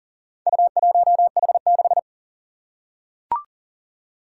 Morse code, heard as a keyed tone, about 40 wpm, 700 Hz. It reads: U1H6
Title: Four Challenging Letter-Number Combinations 40wpm